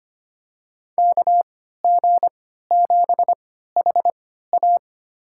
Morse code, heard as a keyed tone, about 25 wpm, spaced as for 14 wpm, 700 Hz. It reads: KG7HA
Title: USA Call Signs - 5 Characters - 3x Character Spacing - 25wpm